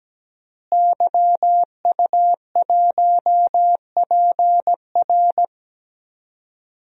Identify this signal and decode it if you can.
Morse code, heard as a keyed tone, about 17 wpm, 700 Hz. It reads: YU1PR
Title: DXCC-100 Call Signs - 5 Characters - Repeated 2x - 17wpm